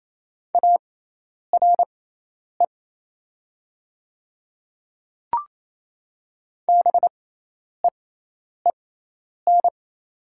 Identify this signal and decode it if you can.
Morse code, heard as a keyed tone, about 28 wpm, spaced as for 10 wpm, 690 Hz. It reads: ARE BEEN
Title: 6x Character Spacing - Top 100 Words - 28wpm